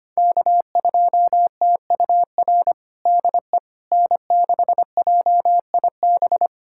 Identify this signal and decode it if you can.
Morse code, heard as a keyed tone, about 25 wpm, 700 Hz. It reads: K2TUR DE N6JIB
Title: USA Callsign DE Callsign - 5 Characters - 25wpm